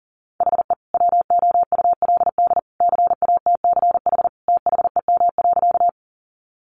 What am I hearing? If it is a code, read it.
Morse code, heard as a keyed tone, about 40 wpm, 700 Hz. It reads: HE WOULD CATCH THEM.
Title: Sentences from Top 1500 Words 40wpm